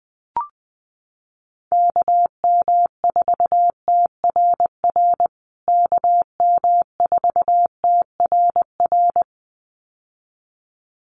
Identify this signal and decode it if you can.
Morse code, heard as a keyed tone, about 20 wpm, 700 Hz. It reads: KM4TRR KM4TRR
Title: USA Call Signs Repeated 2x 20wpm